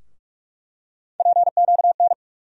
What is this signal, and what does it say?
Morse code, heard as a keyed tone, about 46 wpm, 700 Hz. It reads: PQN
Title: USA Call Sign Postfixes - 3 Letters 45wpm